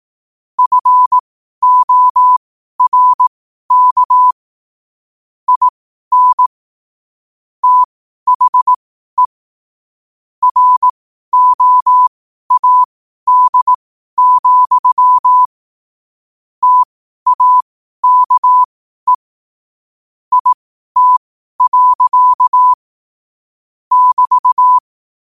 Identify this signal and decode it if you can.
Morse code, heard as a keyed tone, about 18 wpm, spaced as for 12 wpm, 1000 Hz. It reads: FORK IN THE ROAD, TAKE IT. <BT>
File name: qotdfiles / 12wpm